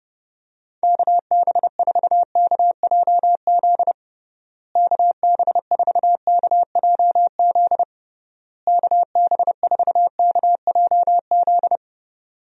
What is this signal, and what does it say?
Morse code, heard as a keyed tone, about 30 wpm, 700 Hz. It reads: KB4KJZ KB4KJZ KB4KJZ
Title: USA Call Signs - 6 Characters - Repeated 3x - 30wpm